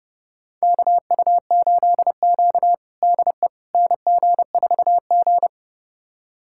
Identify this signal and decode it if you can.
Morse code, heard as a keyed tone, about 30 wpm, 700 Hz. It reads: KU8Q DE NG4G
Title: USA Callsign DE Callsign - 4 Characters - 30wpm